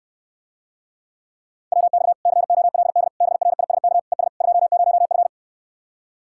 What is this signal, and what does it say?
Morse code, heard as a keyed tone, about 69 wpm, 700 Hz. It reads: WX COLD BRISK ES 20F